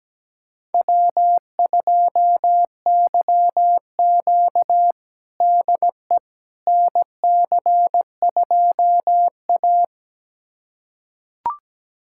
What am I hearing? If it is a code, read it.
Morse code, heard as a keyed tone, about 17 wpm, 700 Hz. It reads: W2YQ DE NC2A
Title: USA Callsign DE Callsign - 4 Characters - 17wpm